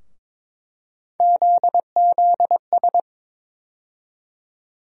Morse code, heard as a keyed tone, about 23 wpm, 700 Hz. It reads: ZZS
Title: USA Call Sign Postfixes - 3 Letters 22wpm